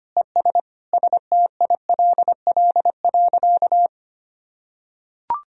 Morse code, heard as a keyed tone, about 25 wpm, 700 Hz. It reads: ES STILL.
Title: Sentences from Top 400 Words and Abbreviations 25wpm